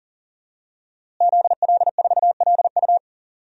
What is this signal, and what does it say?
Morse code, heard as a keyed tone, about 37 wpm, 700 Hz. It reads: ZL4LU
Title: DXCC-100 Call Signs - 5 Characters - Repeated 3x - 40wpm